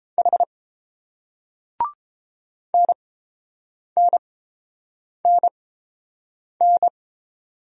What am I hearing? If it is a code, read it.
Morse code, heard as a keyed tone, about 29 wpm, spaced as for 7 wpm, 700 Hz. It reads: H NNNN